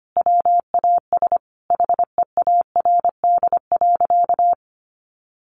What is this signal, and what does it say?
Morse code, heard as a keyed tone, about 25 wpm, 700 Hz. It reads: WAS HEARD.
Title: Sentences from Top 1100 Words 25wpm